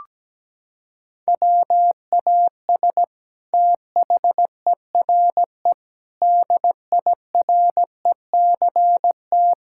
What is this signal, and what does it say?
Morse code, heard as a keyed tone, about 17 wpm, 700 Hz. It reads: WAS THERE DIRECT
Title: Sentences from Top 1000 Words - Review Part 6 - 17wpm